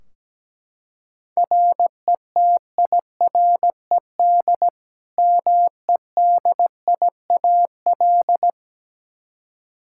Morse code, heard as a keyed tone, about 17 wpm, 700 Hz. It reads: RETIRED MEDIAL
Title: Practice QSO - Exchange 3 - More Info - 17wpm